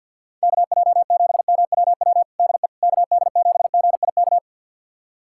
Morse code, heard as a keyed tone, about 48 wpm, 700 Hz. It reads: KJ7MWW DE KD7GIK
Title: USA Callsign DE Callsign - 6 Characters - 50wpm